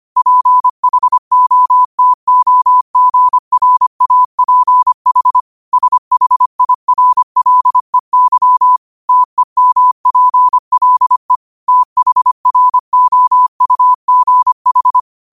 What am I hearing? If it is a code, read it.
Morse code, heard as a keyed tone, about 25 wpm, 1000 Hz. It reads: PHOTOGRAPH SHIRLEY TEMPLE THROUGH
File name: qotdfiles / 25wpm